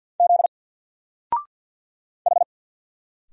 Morse code, heard as a keyed tone, about 48 wpm, 700 Hz. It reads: Z H
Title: Single Character (Letter:Number:Punctuation:Prosign) 50wpm